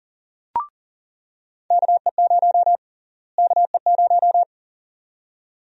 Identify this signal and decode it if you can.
Morse code, heard as a keyed tone, about 40 wpm, 700 Hz. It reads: KE0 KE0
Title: USA Call Sign Prefixes - Repeated 2x - 40wpm